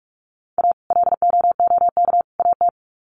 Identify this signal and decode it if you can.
Morse code, heard as a keyed tone, about 44 wpm, 700 Hz. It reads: A LOOK AT
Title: Sets of 3 Words 45wpm